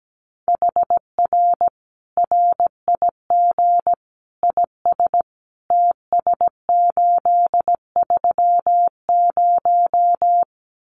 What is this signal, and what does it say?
Morse code, heard as a keyed tone, about 17 wpm, 700 Hz. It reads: HR RIG IS TS830
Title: Common QSO Phrases 17wpm